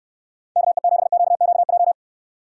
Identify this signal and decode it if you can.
Morse code, heard as a keyed tone, about 69 wpm, 700 Hz. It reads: KE7QQJ